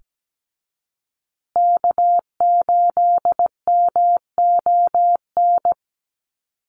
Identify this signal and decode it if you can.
Morse code, heard as a keyed tone, about 17 wpm, 700 Hz. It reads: K8MON
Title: USA Call Signs Repeated 2x 17wpm